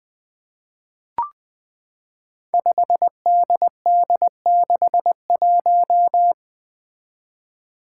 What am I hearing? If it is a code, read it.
Morse code, heard as a keyed tone, about 20 wpm, 700 Hz. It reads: 5DD61
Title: Five Challenging Letter-Number Combinations 20wpm